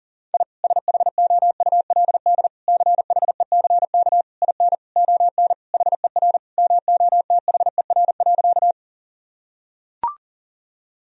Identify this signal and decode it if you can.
Morse code, heard as a keyed tone, about 40 wpm, 700 Hz. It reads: I SHOULD CHECK IN ON HER MOTHER.
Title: Sentences from Top 1000 Words - Review Part 3 - 40wpm